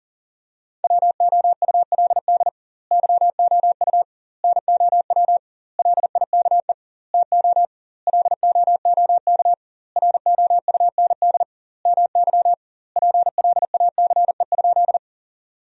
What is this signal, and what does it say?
Morse code, heard as a keyed tone, about 40 wpm, 700 Hz. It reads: WOULD YOU NOW LIKE TO LOOK ROUND MY PLACE?
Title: 2x Word-Spacing - Sentences from Top 200 Words - 40wpm